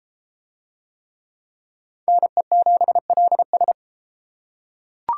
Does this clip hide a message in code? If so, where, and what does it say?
Yes: Morse code from 2.1 to 3.7 s, heard as a keyed tone, about 30 wpm, 700 Hz. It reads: NE7LS